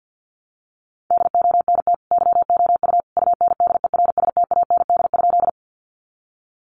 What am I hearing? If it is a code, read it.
Morse code, heard as a keyed tone, about 48 wpm, 700 Hz. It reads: DONT YOU UNDERSTAND?
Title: Sentences from Top 600 Words 50wpm